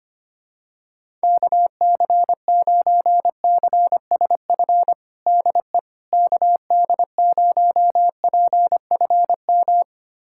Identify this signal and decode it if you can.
Morse code, heard as a keyed tone, about 25 wpm, 700 Hz. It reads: KC9CSF DE KD0PFM
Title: USA Callsign DE Callsign - 6 Characters - 25wpm